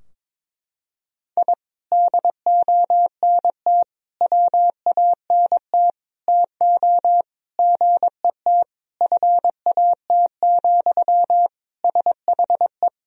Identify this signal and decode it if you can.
Morse code, heard as a keyed tone, about 22 wpm, 700 Hz. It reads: I DONT WANT TO GET FAT, SHE
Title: Sentences from Top 1000 Words - Review Part 4 - 22wpm